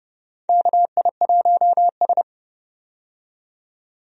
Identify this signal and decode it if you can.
Morse code, heard as a keyed tone, about 30 wpm, 700 Hz. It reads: KI1S
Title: USA Call Signs - 4 Characters - 30wpm